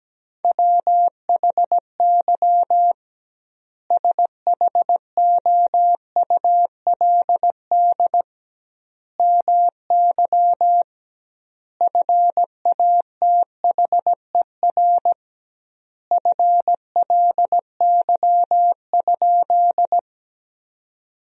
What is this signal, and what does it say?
Morse code, heard as a keyed tone, about 17 wpm, 700 Hz. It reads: WHY SHOULD MY FATHER FLY?
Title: 2x Word-Spacing - Sentences from Top 500 Words - 17wpm